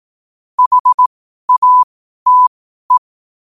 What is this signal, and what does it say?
Morse code, heard as a keyed tone, about 18 wpm, spaced as for 12 wpm, 1000 Hz. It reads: HATE
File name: qotdfiles / 12wpm